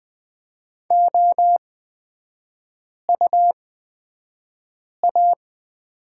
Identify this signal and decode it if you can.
Morse code, heard as a keyed tone, about 20 wpm, spaced as for 5 wpm, 700 Hz. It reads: OUA
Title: 250 Lesson 40e - BK - Three Characters 20wpm